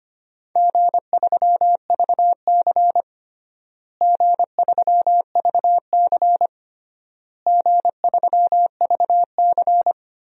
Morse code, heard as a keyed tone, about 25 wpm, 700 Hz. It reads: G3VC G3VC G3VC
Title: DXCC-100 Call Signs - 4 Characters - Repeated 3x - 25wpm